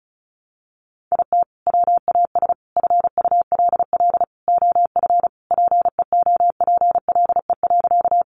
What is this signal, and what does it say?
Morse code, heard as a keyed tone, about 34 wpm, 700 Hz. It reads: IT WAS FULL OF PEOPLE.
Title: Sentences from Top 500 Words 35wpm